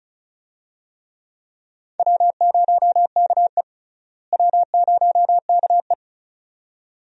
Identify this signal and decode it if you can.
Morse code, heard as a keyed tone, about 34 wpm, 700 Hz. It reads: W0KE W0KE
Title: USA Call Signs - 4 Characters - Repeated 2x - 35wpm